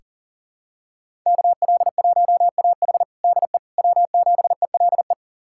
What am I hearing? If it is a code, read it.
Morse code, heard as a keyed tone, about 40 wpm, 700 Hz. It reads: KL1AH DE W7ELE
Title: USA Callsign DE Callsign 40wpm